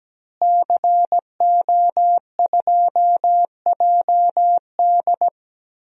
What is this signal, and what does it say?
Morse code, heard as a keyed tone, about 17 wpm, 700 Hz. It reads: CO2JD
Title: DXCC-100 Call Signs - 5 Characters - Repeated 3x - 17wpm